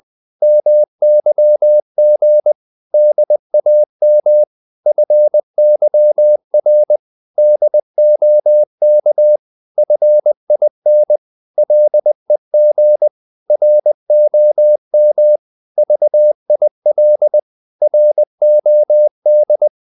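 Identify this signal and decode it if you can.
Morse code, heard as a keyed tone, about 20 wpm, 600 Hz. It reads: MYG DAM FYR DOK FIN LEG ROM VIL ROD
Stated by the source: Morse code practice of short Danish words at 20 wpm, from CW med Gnister.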